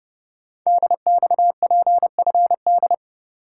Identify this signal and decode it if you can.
Morse code, heard as a keyed tone, about 28 wpm, 700 Hz. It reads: DXPFD
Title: Code Groups - 5 Letters - 30wpm